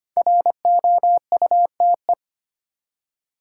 Morse code, heard as a keyed tone, about 25 wpm, 700 Hz. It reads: ROUTE
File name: 5 Letter Words - First 3 Letters Given - 25wpm